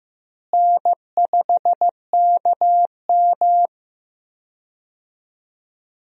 Morse code, heard as a keyed tone, about 15 wpm, 700 Hz. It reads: N5KM
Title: USA Call Signs - 4 Characters - Repeated 3x - 15wpm